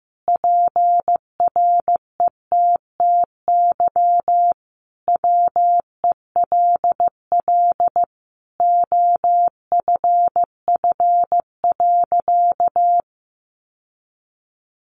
Morse code, heard as a keyed tone, about 14 wpm, 700 Hz. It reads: PRETTY WELL OFF.
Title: Sentences from Top 1000 Words 15wpm